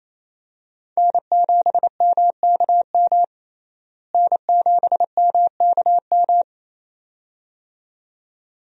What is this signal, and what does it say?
Morse code, heard as a keyed tone, about 28 wpm, 700 Hz. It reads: N7MKM N7MKM
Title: USA Call Signs - 5 Characters - Repeated 2x - 28wpm